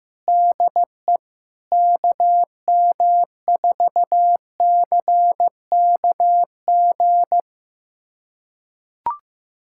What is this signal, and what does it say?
Morse code, heard as a keyed tone, about 15 wpm, 700 Hz. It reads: DE KM4CKG
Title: USA Callsign DE Callsign 15wpm